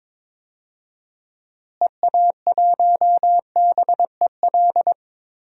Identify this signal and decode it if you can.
Morse code, heard as a keyed tone, about 22 wpm, 700 Hz. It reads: EA1BEL
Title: DXCC-100 Call Signs - 6 Characters - Repeated 2x - 22wpm